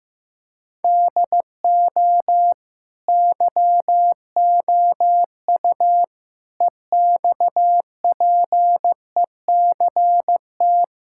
Morse code, heard as a keyed tone, about 15 wpm, 700 Hz. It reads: DO YOU EXPECT
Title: Sentences from Top 1600 Words 15wpm